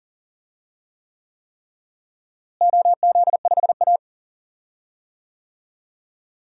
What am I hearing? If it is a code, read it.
Morse code, heard as a keyed tone, about 40 wpm, 700 Hz. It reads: OZ5A
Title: DXCC-100 Call Signs - 4 Characters - Repeated 3x - 40wpm